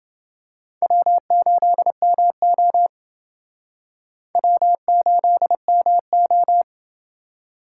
Morse code, heard as a keyed tone, about 29 wpm, 690 Hz. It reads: W8MO W8MO